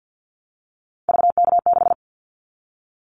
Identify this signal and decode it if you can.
Morse code, heard as a keyed tone, about 50 wpm, 700 Hz. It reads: VK6
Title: Callsign Prefixes for Top 50 DXCC - 50wpm